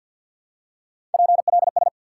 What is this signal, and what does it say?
Morse code, heard as a keyed tone, about 49 wpm, 700 Hz. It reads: PLS
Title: Top 90 QSO Elements 50wpm